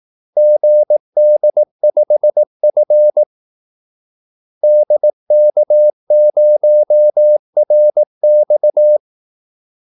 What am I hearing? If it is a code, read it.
Morse code, heard as a keyed tone, about 18 wpm, 600 Hz. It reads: GD5F DK0RX
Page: Kaldesignaler 18 wpm